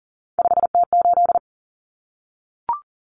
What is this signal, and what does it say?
Morse code, heard as a keyed tone, about 40 wpm, 700 Hz. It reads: 5T8
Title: Common Cut Numbers - 3-Digit 40wpm